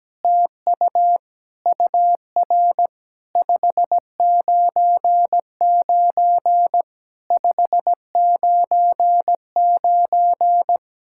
Morse code, heard as a keyed tone, about 17 wpm, 700 Hz. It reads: TU UR 599 599
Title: SOTA QSO Practice 17wpm